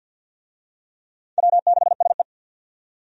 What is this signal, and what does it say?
Morse code, heard as a keyed tone, about 49 wpm, 700 Hz. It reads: W6SE
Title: USA Call Signs - 4 Characters - 50wpm